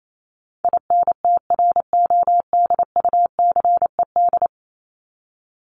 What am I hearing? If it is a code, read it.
Morse code, heard as a keyed tone, about 28 wpm, 700 Hz. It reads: INTRODUCED
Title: ED Suffix - Top 100 Words with Base Words 28wpm